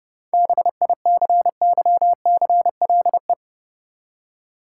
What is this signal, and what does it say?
Morse code, heard as a keyed tone, about 30 wpm, 700 Hz. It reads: BICYCLE
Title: Seven Letter Words - 30wpm